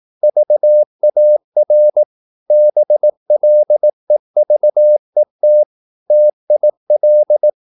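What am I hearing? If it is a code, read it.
Morse code, heard as a keyed tone, about 18 wpm, 600 Hz. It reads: VAR BLEVET TIL
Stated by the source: Slow (QRS) Morse practice at 18 wpm, CW med Gnister.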